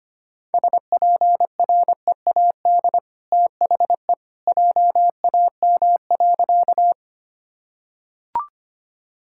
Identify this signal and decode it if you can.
Morse code, heard as a keyed tone, about 25 wpm, 690 Hz. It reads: SPREAD THE JAM.
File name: Easy Sentences with Hints - 25wpm